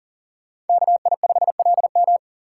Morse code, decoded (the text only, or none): KI5LM